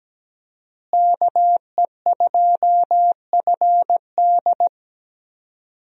Morse code, heard as a keyed tone, about 17 wpm, 700 Hz. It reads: KE2FD
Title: USA Call Signs - 5 Characters - Repeated 2x - 17wpm